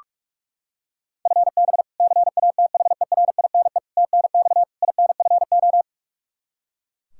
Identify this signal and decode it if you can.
Morse code, heard as a keyed tone, about 41 wpm, 700 Hz. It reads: FB CATHERINE TNX INFO
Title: K1USN SST QSO Practice 45wpm